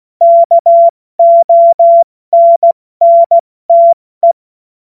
Morse code, heard as a keyed tone, about 16 wpm, 680 Hz. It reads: KONNTE
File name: Morsekurs / Witze / Witz 27. Juli 2021, 70 F80